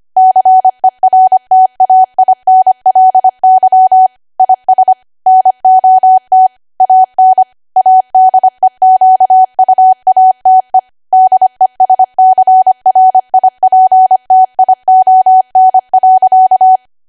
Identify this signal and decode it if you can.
Morse code, heard as a keyed tone, about 25 wpm, 750 Hz. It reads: CERTAINLY IS NOT AN ADEQUATE DESCRIPTION.
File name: cwa / W1AW-Level3 / 25WPM